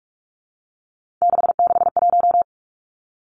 Morse code, heard as a keyed tone, about 46 wpm, 700 Hz. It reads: 661
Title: Three Challenging Letter-Number Combinations 45wpm